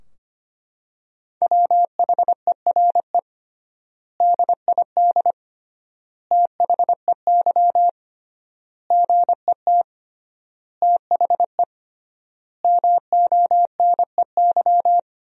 Morse code, heard as a keyed tone, about 25 wpm, 700 Hz. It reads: WHERE DID THEY GET THE MONEY
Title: 3x Word-Spacing - Sentences from Top 400 Words - 25wpm